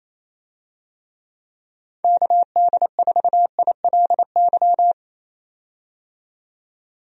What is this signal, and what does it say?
Morse code, heard as a keyed tone, about 28 wpm, 700 Hz. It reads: KD4ILY
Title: USA Call Signs - 6 Characters - 28wpm